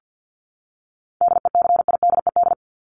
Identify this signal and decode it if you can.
Morse code, heard as a keyed tone, about 50 wpm, 700 Hz. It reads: DECIDED
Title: ED Suffix - Top 100 Words with Base Words 50wpm